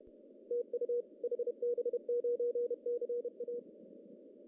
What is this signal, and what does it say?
Morse code, heard as a keyed tone, about 29 wpm, 480 Hz. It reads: TU HB9CA
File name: CW / cqww-cw-k5tr